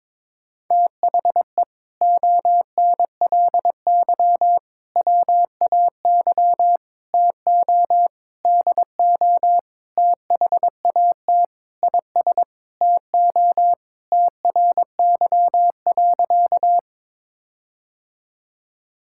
Morse code, heard as a keyed tone, about 22 wpm, 700 Hz. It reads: THE ONLY WAY TO DO THAT IS TO TRY.
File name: practice / 22wpm / Sentences with 8-Word Phrases - 22wpm